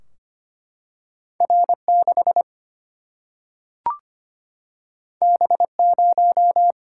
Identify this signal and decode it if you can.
Morse code, heard as a keyed tone, about 25 wpm, 700 Hz. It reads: R6 B0
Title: Single-Letter Single-Number 25wpm